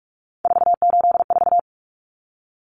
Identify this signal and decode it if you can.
Morse code, heard as a keyed tone, about 44 wpm, 700 Hz. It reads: 484
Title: Common Cut Numbers - 3-Digit 45wpm